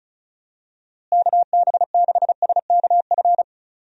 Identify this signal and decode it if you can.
Morse code, heard as a keyed tone, about 34 wpm, 700 Hz. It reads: KB6SKF
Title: USA Call Signs - 6 Characters - Repeated 2x - 35wpm